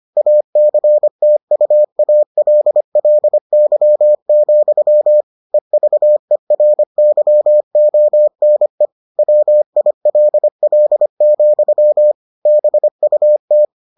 Morse code, heard as a keyed tone, about 25 wpm, 600 Hz. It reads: ACTUALLY, EVERYONE WILL, BUT